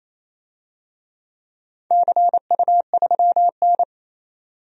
Morse code, heard as a keyed tone, about 28 wpm, 700 Hz. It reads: CU3N
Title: DXCC-100 Call Signs - 4 Characters - 28wpm